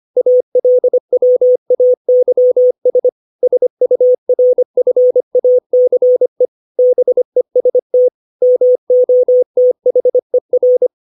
Morse code, heard as a keyed tone, about 25 wpm, 500 Hz. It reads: ALWAYS SURFACE BEST MOTHER
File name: 632 words / 632 Words 25WPM